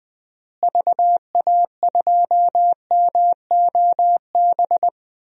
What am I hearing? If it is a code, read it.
Morse code, heard as a keyed tone, about 20 wpm, 700 Hz. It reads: VA2MOB
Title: DXCC-100 Call Signs - 6 Characters - Repeated 3x - 20wpm